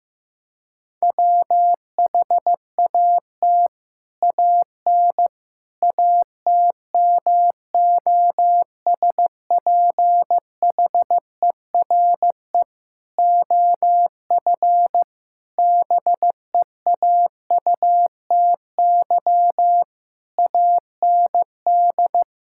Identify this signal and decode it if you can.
Morse code, heard as a keyed tone, about 16 wpm, 700 Hz. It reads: WHAT AN ATMOSPHERE OF BEAUTY AND
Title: Sentences from Top 1600 Words 15wpm